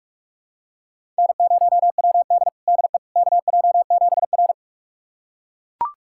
Morse code, heard as a keyed tone, about 41 wpm, 700 Hz. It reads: N0WD DE KJ7R
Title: USA Callsign DE Callsign - 4 Characters - 45wpm